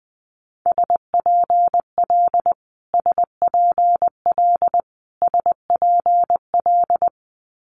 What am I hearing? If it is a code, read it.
Morse code, heard as a keyed tone, about 20 wpm, 700 Hz. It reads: SPL SPL SPL
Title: Multi-letter Phonemes Repeated 3x 20wpm